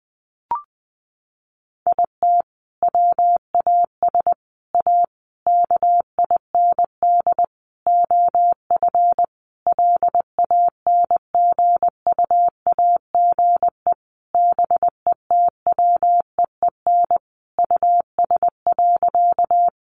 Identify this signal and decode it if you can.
Morse code, heard as a keyed tone, about 20 wpm, 700 Hz. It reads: IT WAS A KIND OF LANGUAGE BETWEEN US.
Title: Sentences from Top 500 Words 20wpm